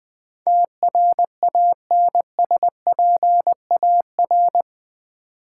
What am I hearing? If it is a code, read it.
Morse code, heard as a keyed tone, about 20 wpm, 700 Hz. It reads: TRANSPAR
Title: Build A Word - 4 Letters At A Time - 12-Letter Words - 20wpm